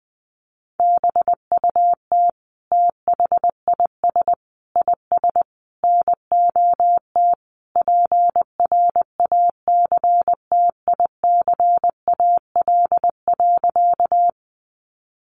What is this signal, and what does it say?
Morse code, heard as a keyed tone, about 20 wpm, 700 Hz. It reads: BUT THIS IS NOT PRACTICAL.
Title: Sentences from Top 1400 Words 20wpm